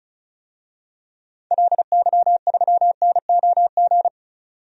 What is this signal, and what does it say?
Morse code, heard as a keyed tone, about 36 wpm, 700 Hz. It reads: LY3NOG
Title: DXCC-100 Call Signs - 6 Characters - 35wpm